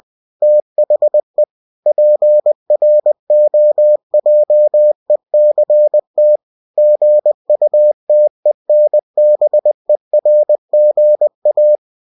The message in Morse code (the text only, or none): THE PROJECT GUTENBERGA